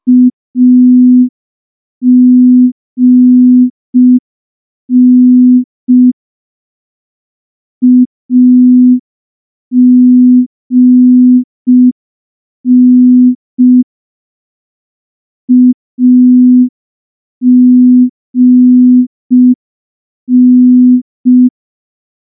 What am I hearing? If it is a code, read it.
Morse code, heard as a keyed tone, about 5 wpm, 250 Hz. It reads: AGN AGN AGN
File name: morse / morse-training-05WPM